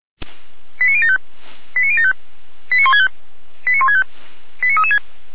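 Five-tone selective call (ZVEI), heard as quick runs of five tones, tones of about 70 ms. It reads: 89075 89075 89165 89165 89336
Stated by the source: ZVEI five-tone selective call 89017 received by SDR.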